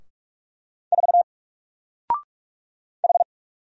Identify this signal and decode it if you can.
Morse code, heard as a keyed tone, about 45 wpm, 700 Hz. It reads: 4 H
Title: Single Letter-Number 45wpm